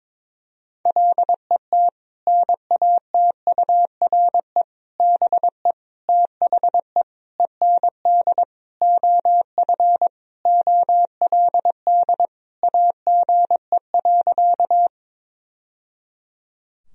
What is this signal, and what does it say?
Morse code, heard as a keyed tone, about 22 wpm, 700 Hz. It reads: LET NATURE BE THE END OF OLD AGE.
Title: Sentences from Top 1000 Words - Review Part 3 - 22wpm